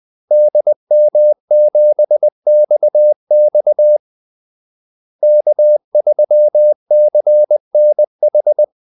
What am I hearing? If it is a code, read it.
Morse code, heard as a keyed tone, about 20 wpm, 600 Hz. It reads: DM7XX K3CNH
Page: Kaldesignaler 20 wpm